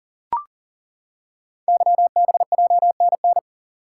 Morse code, heard as a keyed tone, about 40 wpm, 700 Hz. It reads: YBJNN
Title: Code Groups - 5 Letters - 40wpm